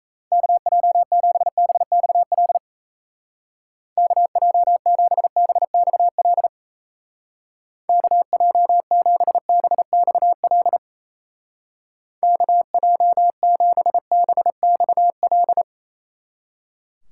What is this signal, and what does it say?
Morse code, heard as a keyed tone, about 34 wpm, 700 Hz. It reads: KJ7BXL KJ7BXL KJ7BXL KJ7BXL
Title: Speed Racing - USA 2x3 Call Signs 28wpm